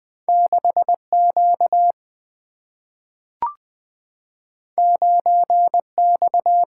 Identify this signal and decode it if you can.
Morse code, heard as a keyed tone, about 20 wpm, 700 Hz. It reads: 6Q 9X
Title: Single-Letter Single-Number 20wpm